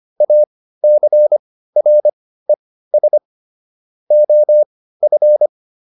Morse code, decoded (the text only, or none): ACRES OF